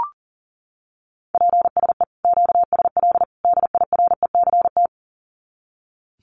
Morse code, heard as a keyed tone, about 40 wpm, 700 Hz. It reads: PSE QSL DIRECT
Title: Common QSO Phrases 40wpm